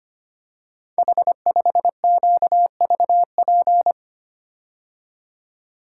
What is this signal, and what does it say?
Morse code, heard as a keyed tone, about 25 wpm, 700 Hz. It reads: H5QVP
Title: Code Groups - 5 Letter:Numbers - 25wpm